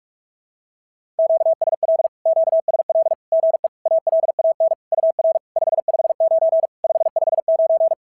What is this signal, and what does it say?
Morse code, heard as a keyed tone, about 44 wpm, 650 Hz. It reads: QSL QSL GE ALAN UR 559 559
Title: SOTA QSO Practice 45wpm